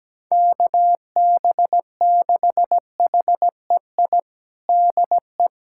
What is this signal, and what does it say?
Morse code, heard as a keyed tone, about 17 wpm, 700 Hz. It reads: KB6HEI DE
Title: USA Callsign DE Callsign - 6 Characters - 17wpm